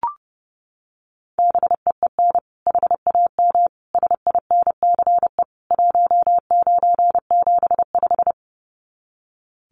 Morse code, heard as a keyed tone, about 30 wpm, 700 Hz. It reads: BEEN HAM SINCE 1975
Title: Common QSO Phrases 30wpm